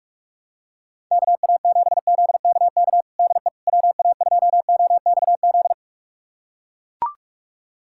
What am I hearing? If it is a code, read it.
Morse code, heard as a keyed tone, about 41 wpm, 700 Hz. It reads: KA7ZKK DE WA2OXZ
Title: USA Callsign DE Callsign - 6 Characters - 45wpm